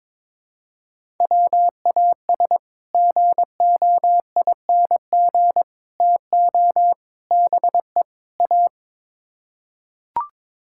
Morse code, heard as a keyed tone, about 22 wpm, 700 Hz. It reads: WAS GOING TO BE A
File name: practice / 22wpm / Build A Phrase - 5-Word Phrases - 22wpm